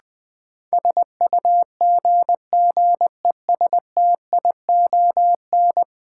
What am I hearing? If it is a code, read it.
Morse code, heard as a keyed tone, about 20 wpm, 700 Hz. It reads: SUGGESTION
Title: Sentences from Top 1600 Words - New Word Drill Alt - 20wpm